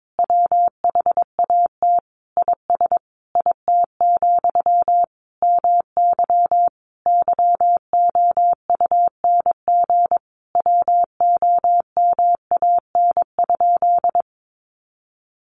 Morse code, heard as a keyed tone, about 22 wpm, 700 Hz. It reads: WHAT IS IT, MY YOUNG WOMAN?
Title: Sentences from Top 1100 Words 22wpm